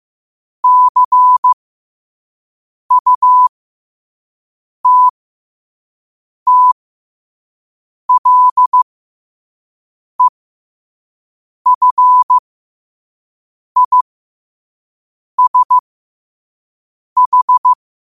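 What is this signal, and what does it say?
Morse code, heard as a keyed tone, about 15 wpm, spaced as for 5 wpm, 1000 Hz. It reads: CUTTLEFISH
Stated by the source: Morse code sent at 5 wpm.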